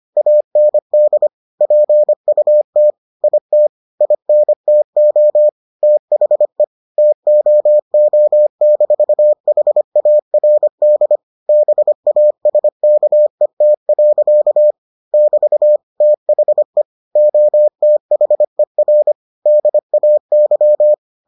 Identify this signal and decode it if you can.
Morse code, heard as a keyed tone, about 25 wpm, 600 Hz. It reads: AND PUT IT INTO THE TOO-HARD BASKET. <BT> THE OTHER DAY